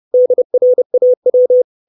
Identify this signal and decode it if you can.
Morse code, heard as a keyed tone, about 30 wpm, 500 Hz. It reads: DRAW
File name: 632 Words 30WPM